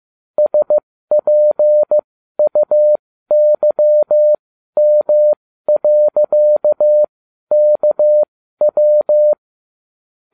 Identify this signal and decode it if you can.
Morse code, heard as a keyed tone, about 15 wpm, spaced as for 12 wpm, 600 Hz. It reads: SPUYM.KW